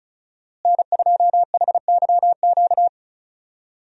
Morse code, heard as a keyed tone, about 34 wpm, 700 Hz. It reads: N2HYQ
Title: USA Call Signs - 5 Characters - Repeated 2x - 35wpm